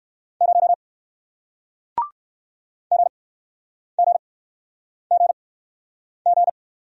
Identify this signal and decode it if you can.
Morse code, heard as a keyed tone, about 57 wpm, spaced as for 9 wpm, 700 Hz. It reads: 0 GGGG